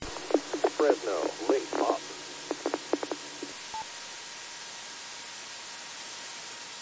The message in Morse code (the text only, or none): none